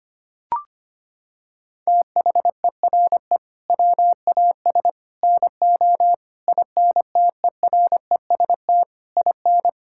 Morse code, heard as a keyed tone, about 25 wpm, 700 Hz. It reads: THERE WAS NO INTEREST IN
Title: Sentences from Top 1000 Words - Review Part 5 - 25wpm